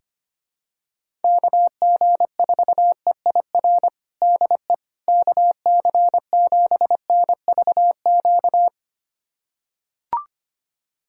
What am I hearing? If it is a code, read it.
Morse code, heard as a keyed tone, about 25 wpm, 700 Hz. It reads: KG4EIR DE KC7NVQ
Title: USA Callsign DE Callsign 25wpm